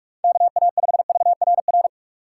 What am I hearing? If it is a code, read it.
Morse code, heard as a keyed tone, about 43 wpm, 700 Hz. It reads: KA5VRR
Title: USA Call Signs - 6 Characters - 6x Character Spacing - 45wpm